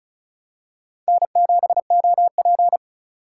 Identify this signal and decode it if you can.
Morse code, heard as a keyed tone, about 36 wpm, 700 Hz. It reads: N7OP
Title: USA Call Signs - 4x Character Spacing - 35wpm